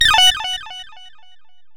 There is no code.